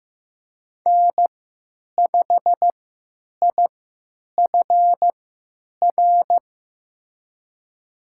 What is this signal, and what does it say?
Morse code, heard as a keyed tone, about 15 wpm, spaced as for 9 wpm, 700 Hz. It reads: N5IFR
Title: USA Call Signs - 5 Characters - 3x Character Spacing - 15wpm